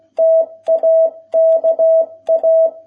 Morse code, heard as a keyed tone, about 16 wpm, 650 Hz. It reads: TAKA